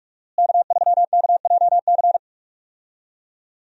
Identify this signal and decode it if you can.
Morse code, heard as a keyed tone, about 43 wpm, 700 Hz. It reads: K3KJC
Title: USA Call Signs - 5 Characters - Repeated 2x - 45wpm